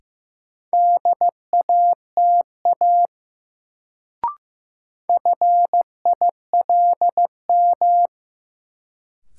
Morse code, heard as a keyed tone, about 15 wpm, 700 Hz. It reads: DATA FILM
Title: Sentences from Top 1000 Words - Review Part 3 - New Word Drill - 15wpm